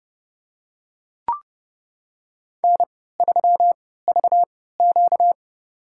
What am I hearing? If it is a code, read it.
Morse code, heard as a keyed tone, about 29 wpm, spaced as for 17 wpm, 700 Hz. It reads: N3VQ
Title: USA Call Signs - 4 Characters - 3x Character Spacing - 30wpm